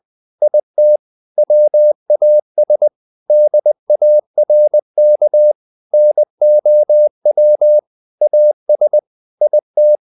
Morse code, heard as a keyed tone, about 20 wpm, 600 Hz. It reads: IT WAS DARK NOW AS IT
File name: the old man and the sea part 2 20wpm